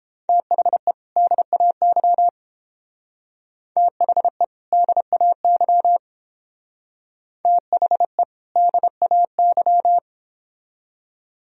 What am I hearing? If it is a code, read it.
Morse code, heard as a keyed tone, about 29 wpm, 700 Hz. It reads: THE DAY THE DAY THE DAY